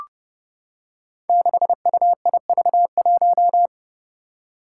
Morse code, heard as a keyed tone, about 30 wpm, 700 Hz. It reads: 6UIV1
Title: Code Groups - 5 Letter:Numbers - 30wpm